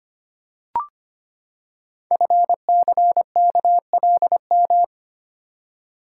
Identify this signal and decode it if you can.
Morse code, heard as a keyed tone, about 25 wpm, 700 Hz. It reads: FCKLM
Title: Code Groups - 5 Letters - 25wpm